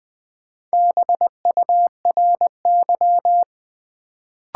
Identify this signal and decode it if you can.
Morse code, heard as a keyed tone, about 20 wpm, 700 Hz. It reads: BURY
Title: Build A Word - 1 Letter At A Time - 4-Letter Words - 20wpm